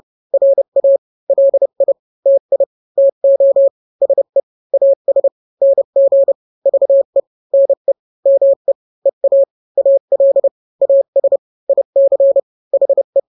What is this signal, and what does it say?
Morse code, heard as a keyed tone, about 30 wpm, 550 Hz. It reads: RA LI TI TO SE AS NG VE NE ME EA AL AS IC HE
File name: n-grams / 2-grams 30 25 WPM